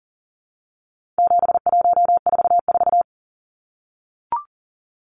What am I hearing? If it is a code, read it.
Morse code, heard as a keyed tone, about 40 wpm, 700 Hz. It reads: 7144
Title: Numbers - 4-Digit 40wpm